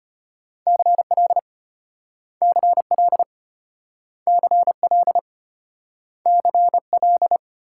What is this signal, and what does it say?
Morse code, heard as a keyed tone, about 32 wpm, 700 Hz. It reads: CL CL CL CL